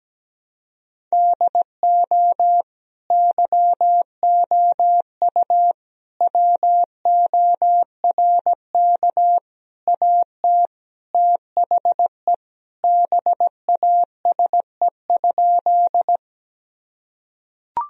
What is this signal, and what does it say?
Morse code, heard as a keyed tone, about 17 wpm, 700 Hz. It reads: DO YOU WORK AT THE BASE?
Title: Sentences from Top 1000 Words - Review Part 1 - 17wpm